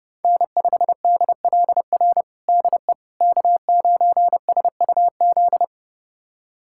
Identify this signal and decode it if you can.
Morse code, heard as a keyed tone, about 30 wpm, 700 Hz. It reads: N5DLR DE K9SUZ
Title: USA Callsign DE Callsign - 5 Characters - 30wpm